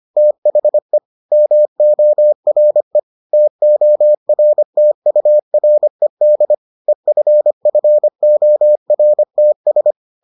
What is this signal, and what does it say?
Morse code, heard as a keyed tone, about 25 wpm, 600 Hz. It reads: THE MORE TORTURED EFFORTS